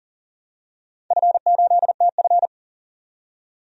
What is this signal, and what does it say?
Morse code, heard as a keyed tone, about 41 wpm, 700 Hz. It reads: F8TF
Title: DXCC-100 Call Signs - 4 Characters - 40wpm